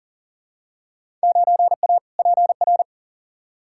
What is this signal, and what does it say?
Morse code, heard as a keyed tone, about 40 wpm, 700 Hz. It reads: 9A PR
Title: Field Day Exchanges Encore - 40wpm